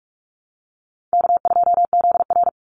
Morse code, heard as a keyed tone, about 44 wpm, 700 Hz. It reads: K2ZR
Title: Easy Call Signs - 45wpm